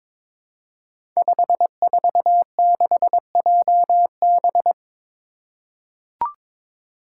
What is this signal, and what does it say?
Morse code, heard as a keyed tone, about 22 wpm, 700 Hz. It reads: 546JB
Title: Five Challenging Letter-Number Combinations 22wpm